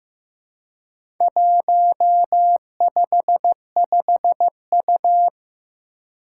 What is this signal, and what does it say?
Morse code, heard as a keyed tone, about 15 wpm, 700 Hz. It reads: 155U
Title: Four Challenging Letter-Number Combinations 15wpm